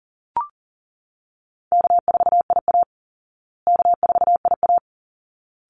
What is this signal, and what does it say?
Morse code, heard as a keyed tone, about 37 wpm, 700 Hz. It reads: K4IA K4IA
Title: Easy Call Signs Repeated 2x - 40wpm